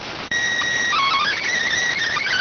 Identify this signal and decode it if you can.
SSTV (slow-scan television), PD 290.